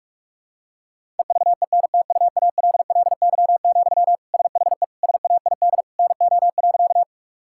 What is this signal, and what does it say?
Morse code, heard as a keyed tone, about 41 wpm, 700 Hz. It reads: EVENTUALLY, SHE SAID NO.
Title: Sentences from Top 1000 Words - Review Part 6 - 45wpm